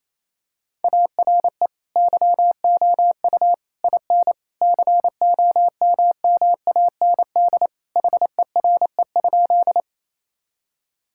Morse code, heard as a keyed tone, about 28 wpm, 700 Hz. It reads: ARE YOU IN COMMAND HERE?
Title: Sentences from Top 1600 Words 28wpm